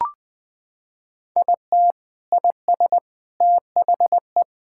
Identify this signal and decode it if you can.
Morse code, heard as a keyed tone, about 20 wpm, 700 Hz. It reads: IT IS THE
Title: Sentences from Top 1600 Words 20wpm